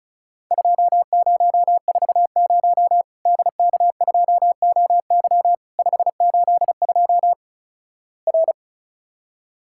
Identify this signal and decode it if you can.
Morse code, heard as a keyed tone, about 36 wpm, 700 Hz. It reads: 2040 DK2OY 582 R
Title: Worked All Europe DX Contest - QTC Exchange - 35wpm